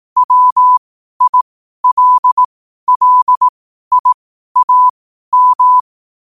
Morse code, heard as a keyed tone, about 18 wpm, spaced as for 12 wpm, 1000 Hz. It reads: WILLIAM
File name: qotdfiles / 12wpm